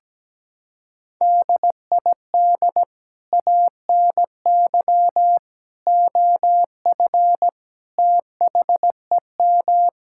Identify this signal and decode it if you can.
Morse code, heard as a keyed tone, about 17 wpm, 700 Hz. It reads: DID ANY OF THEM
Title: Sentences from Top 1600 Words 17wpm